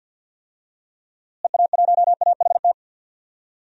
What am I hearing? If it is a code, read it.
Morse code, heard as a keyed tone, about 52 wpm, 700 Hz. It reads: EA1AHT
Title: DXCC-100 Call Signs - 6 Characters - Repeated 2x - 50wpm